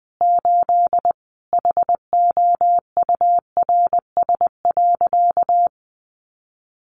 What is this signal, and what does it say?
Morse code, heard as a keyed tone, about 20 wpm, 700 Hz. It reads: 8 HOURS.
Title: Additional Easy Phrases 20wpm